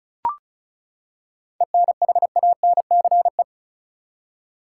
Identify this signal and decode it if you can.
Morse code, heard as a keyed tone, about 36 wpm, 700 Hz. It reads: ENHANCE
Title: Seven Letter Words - 35wpm